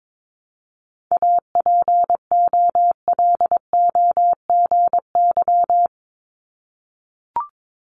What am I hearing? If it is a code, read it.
Morse code, heard as a keyed tone, about 22 wpm, 700 Hz. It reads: APOLOGY
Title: Multi-letter Phonemes Repeated 3x 22wpm